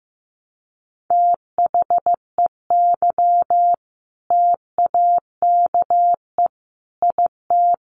Morse code, heard as a keyed tone, about 15 wpm, 700 Hz. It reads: THEY TAKE IT
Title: Sentences from Top 1200 Words 15wpm